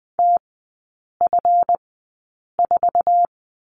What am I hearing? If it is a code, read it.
Morse code, heard as a keyed tone, about 20 wpm, spaced as for 8 wpm, 700 Hz. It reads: TF4
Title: 138 Lesson 23e - Letter f - Three Characters 20wpm